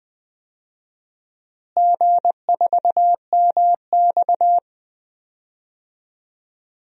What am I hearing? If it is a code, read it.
Morse code, heard as a keyed tone, about 20 wpm, 700 Hz. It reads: G4MX
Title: DXCC-100 Call Signs - 4 Characters - Repeated 3x - 20wpm